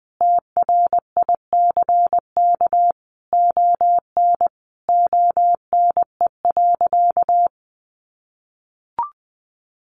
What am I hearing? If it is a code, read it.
Morse code, heard as a keyed tone, about 20 wpm, 700 Hz. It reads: TRICK ON ONE.
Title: Sentences from Top 1500 Words 20wpm